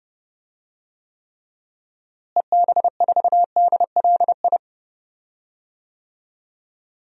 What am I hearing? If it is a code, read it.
Morse code, heard as a keyed tone, about 32 wpm, 700 Hz. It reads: EB4DLI
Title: DXCC-100 Call Signs - 6 Characters - Repeated 2x - 30wpm